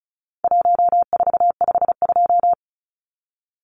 Morse code, heard as a keyed tone, about 34 wpm, 700 Hz. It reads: 1452
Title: Numbers - Contest Style - 4-Digit 35wpm